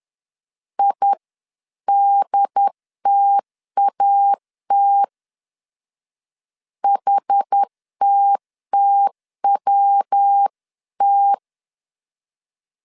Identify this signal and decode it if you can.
Morse code, heard as a keyed tone, about 10 wpm, 800 Hz. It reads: IDTAT HTTWT
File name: 10wpm